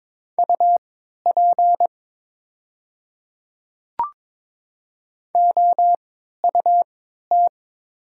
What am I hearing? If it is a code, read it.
Morse code, heard as a keyed tone, about 22 wpm, spaced as for 12 wpm, 700 Hz. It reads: UP OUT
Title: 3x Character Spacing - Top 100 Words - 22wpm